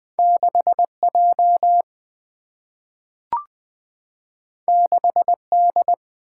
Morse code, heard as a keyed tone, about 20 wpm, 700 Hz. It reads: 6J 6D
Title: Single-Letter Single-Number 20wpm